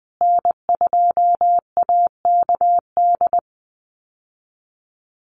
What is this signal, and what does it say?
Morse code, heard as a keyed tone, about 20 wpm, 700 Hz. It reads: N2AKD
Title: USA Call Signs Repeated 3x 20wpm